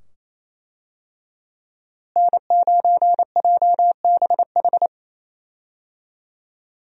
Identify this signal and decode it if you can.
Morse code, heard as a keyed tone, about 27 wpm, 700 Hz. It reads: N9JBH
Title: USA Call Signs - 5 Characters - Repeated 2x - 28wpm